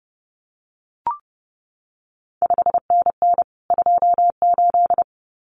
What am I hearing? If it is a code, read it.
Morse code, heard as a keyed tone, about 30 wpm, 700 Hz. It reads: 5NN 28
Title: CQ WW DX Contest Exchange Encore - 30wpm